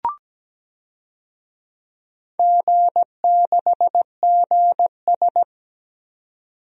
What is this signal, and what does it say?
Morse code, heard as a keyed tone, about 17 wpm, 700 Hz. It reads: G6GS
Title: DXCC-100 Call Signs - 4 Characters - 17wpm